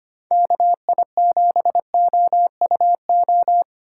Morse code, heard as a keyed tone, about 25 wpm, 700 Hz. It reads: KI7OUO
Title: USA Call Signs - 4x Character Spacing - 25wpm